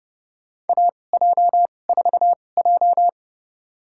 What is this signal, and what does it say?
Morse code, heard as a keyed tone, about 30 wpm, spaced as for 22 wpm, 700 Hz. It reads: AJ4J
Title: USA Call Signs - 4 Characters - 2x Character Spacing - 30wpm